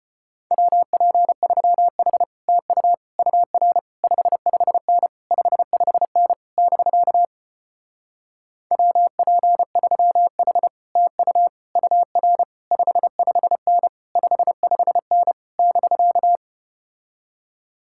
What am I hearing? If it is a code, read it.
Morse code, heard as a keyed tone, about 33 wpm, 700 Hz. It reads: WP3H TU UR 55N 55N <BK> WP3H TU UR 55N 55N <BK>